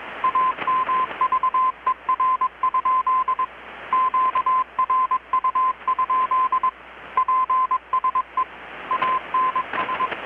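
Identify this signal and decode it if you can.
Morse code, heard as a keyed tone, about 21 wpm, 1030 Hz. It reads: AMVER? QRU? PSE ANS
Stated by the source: Morse from KPH's Night of Nights XX maritime radiotelegraphy event, received in Chiapas, Mexico about 3,800 km away on a Kenwood R-600 with a grounded 10 m long-wire antenna.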